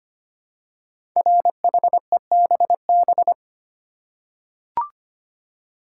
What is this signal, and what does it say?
Morse code, heard as a keyed tone, about 25 wpm, 700 Hz. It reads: RHEBB
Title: Code Groups - 5 Letters - 25wpm